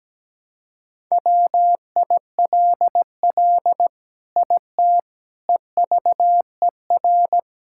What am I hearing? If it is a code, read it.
Morse code, heard as a keyed tone, about 17 wpm, 700 Hz. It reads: WILL IT EVER
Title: Sentences from Top 1000 Words - Review Part 5 - 17wpm